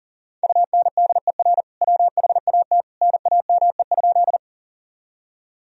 Morse code, heard as a keyed tone, about 40 wpm, 700 Hz. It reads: UNDER WHAT NAME?
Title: Sentences from Top 1000 Words - Review Part 1 - 40wpm